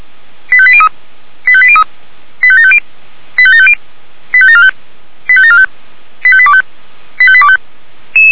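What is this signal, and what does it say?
Five-tone selective call (ZVEI), heard as quick runs of five tones, tones of about 70 ms. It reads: 85803 85803 85750 85750 85745 85745 85825 85825
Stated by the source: ZVEI five-tone selective call received by SDR.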